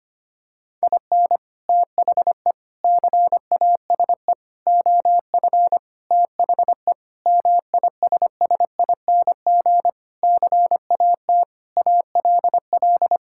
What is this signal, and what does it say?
Morse code, heard as a keyed tone, about 25 wpm, 700 Hz. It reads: IN THE CASE OF THE MISSING CAT ALL
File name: practice / 25wpm / Sentences with 4-Word Phrases - 25wpm